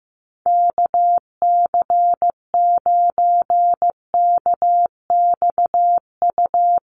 Morse code, heard as a keyed tone, about 15 wpm, 700 Hz. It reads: KC9KXU
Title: USA Call Signs Repeated 3x 15wpm